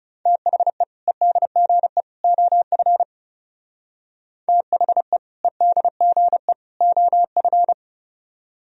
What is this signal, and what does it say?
Morse code, heard as a keyed tone, about 36 wpm, 700 Hz. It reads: THE EDGE OF THE EDGE OF
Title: Speed-Racing - Sets of 3 Words 30wpm